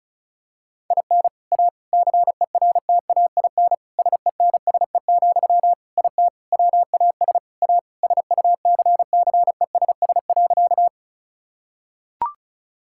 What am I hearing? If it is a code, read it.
Morse code, heard as a keyed tone, about 36 wpm, 700 Hz. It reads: IN A CERTAIN SENSE, IT WAS A SUCCESS.
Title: Sentences from Top 1000 Words - Review Part 6 - 35wpm